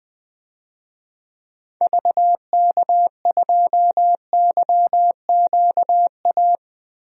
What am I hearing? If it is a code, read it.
Morse code, heard as a keyed tone, about 20 wpm, 700 Hz. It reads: VK2YQA
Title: DXCC-100 Call Signs - 6 Characters - Repeated 2x - 20wpm